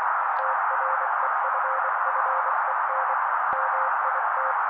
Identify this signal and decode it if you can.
Morse code, heard as a keyed tone, about 24 wpm, 530 Hz. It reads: TREFFEN MIT
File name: daily / 23wpm noise